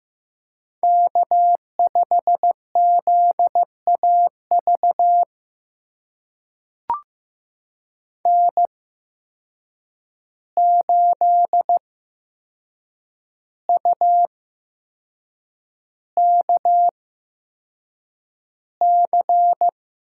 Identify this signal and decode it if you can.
Morse code, heard as a keyed tone, about 15 wpm, spaced as for 6 wpm, 700 Hz. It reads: K5ZAV N8UKC
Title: USA Call Signs - 8x Character Spacing - 15wpm